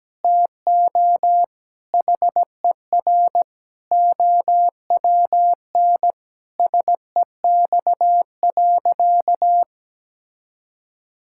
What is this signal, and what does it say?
Morse code, heard as a keyed tone, about 17 wpm, 700 Hz. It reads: TO HER OWN SEX.
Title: Sentences from Top 1600 Words 17wpm